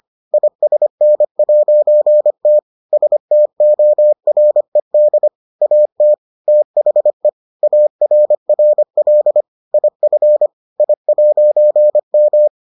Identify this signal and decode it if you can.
Morse code, heard as a keyed tone, about 25 wpm, 600 Hz. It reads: ISN'T STORED AT THE ARRL IF I'M